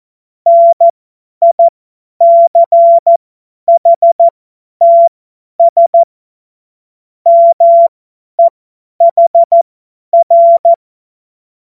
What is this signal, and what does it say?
Morse code, heard as a keyed tone, about 14 wpm, spaced as for 10 wpm, 680 Hz. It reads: NICHTS MEHR
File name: Witz-29-Juli-2025-50-F70